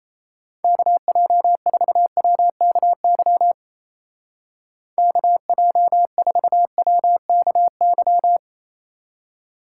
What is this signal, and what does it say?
Morse code, heard as a keyed tone, about 34 wpm, 700 Hz. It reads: KJ4WKY KJ4WKY
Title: Speed Racing - USA 2x3 Call Signs 28wpm